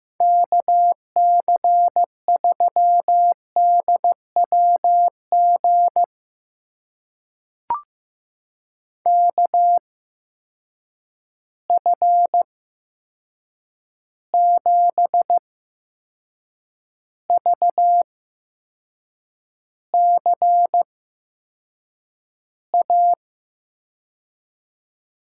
Morse code, heard as a keyed tone, about 15 wpm, spaced as for 6 wpm, 700 Hz. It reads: KC3DWG KF7VCA
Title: USA Call Signs - 8x Character Spacing - 15wpm